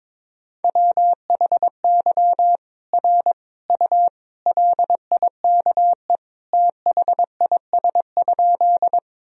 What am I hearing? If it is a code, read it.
Morse code, heard as a keyed tone, about 22 wpm, 700 Hz. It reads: WHY R U LIKE THIS?
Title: Sentences from Top 200 Words and Abbreviations 22wpm